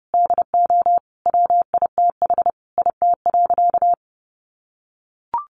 Morse code, heard as a keyed tone, about 28 wpm, 700 Hz. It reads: DO WITH IT.
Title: Sentences from Top 1100 Words 30wpm